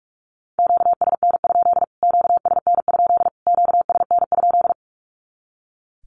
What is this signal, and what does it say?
Morse code, heard as a keyed tone, about 44 wpm, 700 Hz. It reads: QSN? QSN? QSN?
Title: Q-Codes Repeated 3x 45wpm